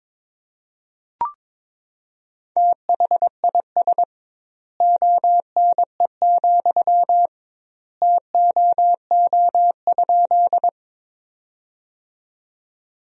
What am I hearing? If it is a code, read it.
Morse code, heard as a keyed tone, about 22 wpm, 700 Hz. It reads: THIS ONE, TOO?
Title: 2x Word-Spacing - Sentences from Top 200 Words - 22wpm